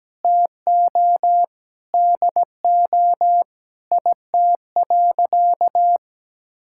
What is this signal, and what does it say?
Morse code, heard as a keyed tone, about 17 wpm, 700 Hz. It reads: TO DO IT.
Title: Sentences from Top 1000 Words - Review Part 5 - 17wpm